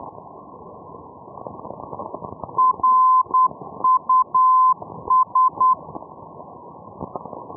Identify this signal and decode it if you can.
Morse code, heard as a keyed tone, about 10 wpm, 1030 Hz. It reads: RUS
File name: europe / e-314 ndb reus rus-424-1305-day-menorca-rb